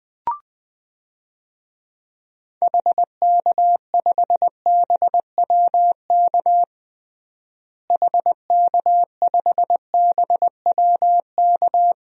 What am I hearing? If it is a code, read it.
Morse code, heard as a keyed tone, about 20 wpm, 700 Hz. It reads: HK5BWK HK5BWK
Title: DXCC-100 Call Signs - 6 Characters - Repeated 3x - 20wpm